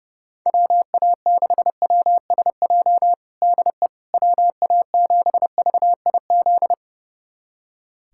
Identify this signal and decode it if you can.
Morse code, heard as a keyed tone, about 30 wpm, 700 Hz. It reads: WA6WSJ DE WA7VIZ
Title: USA Callsign DE Callsign - 6 Characters - 30wpm